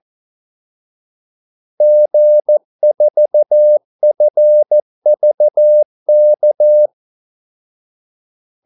Morse code, heard as a keyed tone, about 14 wpm, 600 Hz. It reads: G4FVK
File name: Callsigns-14wpm